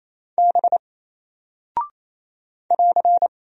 Morse code, heard as a keyed tone, about 27 wpm, 700 Hz. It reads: B <AR>
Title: Single Character (Letter:Number:Punctuation:Prosign) 28wpm